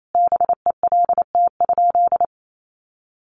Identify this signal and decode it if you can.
Morse code, heard as a keyed tone, about 28 wpm, 700 Hz. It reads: BELT?
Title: Sentences from Top 1400 Words 28wpm